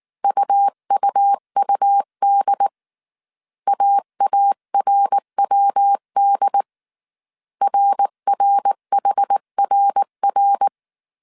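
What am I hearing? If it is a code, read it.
Morse code, heard as a keyed tone, about 19 wpm, 800 Hz. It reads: UUUD AARWD RRHRR
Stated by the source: Morse code dictation at 18 wpm from the IW3SGT QRS Morse code training course, five-character groups of 300 characters.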